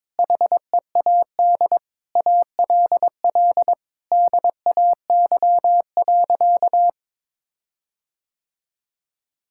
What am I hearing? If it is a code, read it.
Morse code, heard as a keyed tone, about 22 wpm, 700 Hz. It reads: HEAD ALL DAY.
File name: Difficult Sentences with Hints - 22wpm